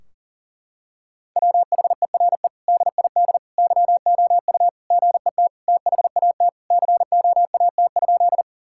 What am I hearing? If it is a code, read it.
Morse code, heard as a keyed tone, about 37 wpm, 700 Hz. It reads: WHERE DID YOU GET THAT COAT?
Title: Sentences from Top 1600 Words 40wpm